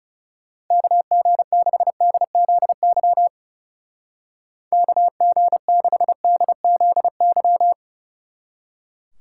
Morse code, heard as a keyed tone, about 38 wpm, 700 Hz. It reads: KG6DZY KG6DZY